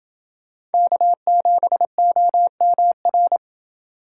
Morse code, heard as a keyed tone, about 29 wpm, 700 Hz. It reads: K7OMR